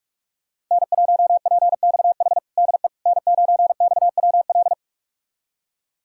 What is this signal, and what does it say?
Morse code, heard as a keyed tone, about 46 wpm, 700 Hz. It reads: N1PXH DE N9XWL
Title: USA Callsign DE Callsign - 5 Characters - 45wpm